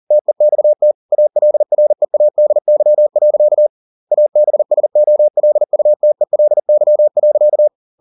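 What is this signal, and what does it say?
Morse code, heard as a keyed tone, about 40 wpm, 600 Hz. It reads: TEXT ALREADY. ABSOLUTELY.
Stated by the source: Morse code at 40 wpm, practice text, CW med Gnister.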